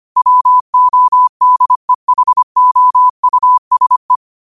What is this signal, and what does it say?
Morse code, heard as a keyed tone, about 25 wpm, 1000 Hz. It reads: WODEHOUSE